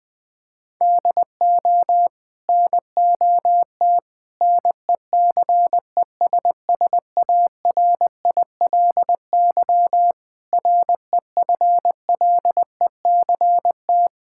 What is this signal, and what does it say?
Morse code, heard as a keyed tone, about 21 wpm, 700 Hz. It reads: DO NOT NECESSARILY REFLECT
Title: Sets of 8 Words - 20wpm